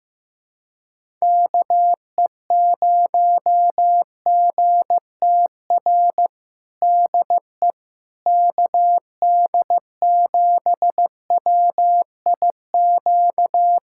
Morse code, heard as a keyed tone, about 15 wpm, 700 Hz. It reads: KE0GTR DE KD7WIQ
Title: USA Callsign DE Callsign - 6 Characters - 15wpm